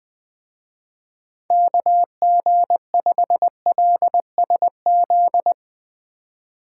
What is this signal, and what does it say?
Morse code, heard as a keyed tone, about 20 wpm, 700 Hz. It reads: KG5LSZ
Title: Speed Racing - USA 2x3 Call Signs 17wpm